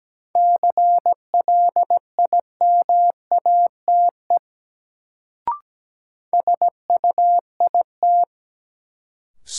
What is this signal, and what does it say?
Morse code, heard as a keyed tone, about 17 wpm, 700 Hz. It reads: CLIMATE SUIT
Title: Sentences from Top 1600 Words - New Word Drill Alt - 17wpm